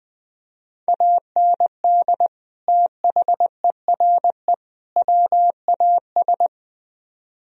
Morse code, heard as a keyed tone, about 20 wpm, 700 Hz. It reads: AND THERE WAS
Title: Speed-Racing - Sets of 3 Words 17wpm